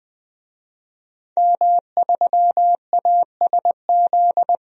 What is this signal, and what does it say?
Morse code, heard as a keyed tone, about 20 wpm, 700 Hz. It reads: M3ASZ
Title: DXCC-100 Call Signs - 5 Characters - 20wpm